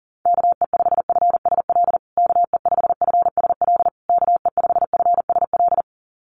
Morse code, heard as a keyed tone, about 40 wpm, 700 Hz. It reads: KE5FSL KE5FSL KE5FSL
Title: USA Call Signs Repeated 3x 40wpm